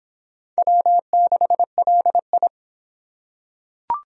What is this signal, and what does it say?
Morse code, heard as a keyed tone, about 23 wpm, 700 Hz. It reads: W6LI